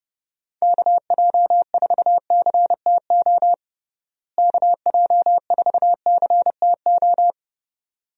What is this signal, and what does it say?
Morse code, heard as a keyed tone, about 30 wpm, 700 Hz. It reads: KJ4CTO KJ4CTO
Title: USA Call Signs - 6 Characters - Repeated 2x - 30wpm